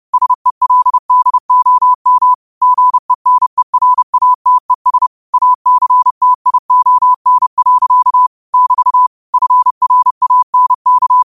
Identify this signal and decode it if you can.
Morse code, heard as a keyed tone, about 30 wpm, 1000 Hz. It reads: SELDOM GENERATES ACTION. <BT> FRANK
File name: qotdfiles / 30wpm